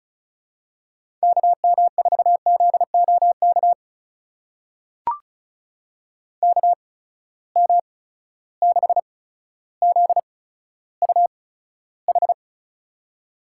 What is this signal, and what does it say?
Morse code, heard as a keyed tone, about 36 wpm, spaced as for 15 wpm, 700 Hz. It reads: KM4ZOK KM6ZUH
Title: USA Call Signs - 6 Characters - 8x Character Spacing - 35wpm